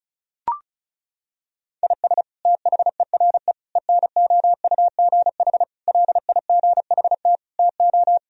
Morse code, heard as a keyed tone, about 36 wpm, 700 Hz. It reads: IS THERE ENOUGH LIGHT TO
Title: Sentences from Top 1000 Words - Review Part 1 - 35wpm